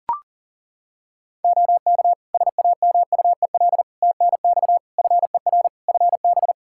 Morse code, heard as a keyed tone, about 40 wpm, 700 Hz. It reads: OK SAMUEL TNX FER FB
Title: K1USN SST QSO Practice 40wpm